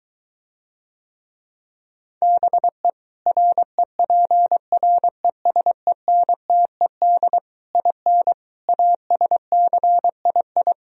Morse code, heard as a keyed tone, about 23 wpm, 700 Hz. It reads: BE REPRESENTED IN ASCII